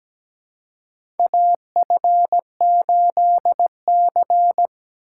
Morse code, heard as a keyed tone, about 17 wpm, 700 Hz. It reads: AF8C
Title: USA Call Signs - 4 Characters - 3x Character Spacing - 17wpm